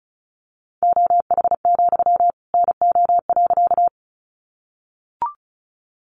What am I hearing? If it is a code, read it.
Morse code, heard as a keyed tone, about 34 wpm, 700 Hz. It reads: OH, NO.
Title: Sentences from Top 500 Words 35wpm